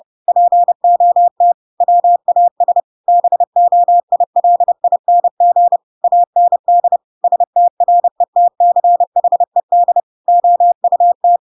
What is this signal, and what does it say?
Morse code, heard as a keyed tone, about 30 wpm, 700 Hz. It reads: POT WAS BOILING AND STRETCHED OUT